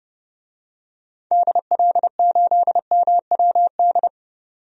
Morse code, heard as a keyed tone, about 32 wpm, 700 Hz. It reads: DL8MWD
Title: DXCC-100 Call Signs - 6 Characters - 30wpm